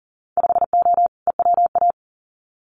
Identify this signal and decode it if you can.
Morse code, heard as a keyed tone, about 40 wpm, 700 Hz. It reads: 5O EWA
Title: Winter Field Day Exchanges Encore 40wpm